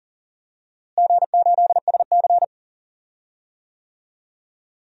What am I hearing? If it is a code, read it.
Morse code, heard as a keyed tone, about 39 wpm, 700 Hz. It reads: G8SC
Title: DXCC-100 Call Signs - 4 Characters - Repeated 3x - 40wpm